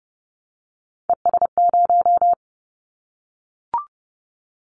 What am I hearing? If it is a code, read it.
Morse code, heard as a keyed tone, about 30 wpm, 700 Hz. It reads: ES0
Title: Callsign Prefixes for Top 100 DXCC - 30wpm